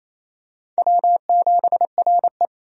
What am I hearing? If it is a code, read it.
Morse code, heard as a keyed tone, about 28 wpm, 700 Hz. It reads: W7RE
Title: USA Call Signs - 4 Characters - 28wpm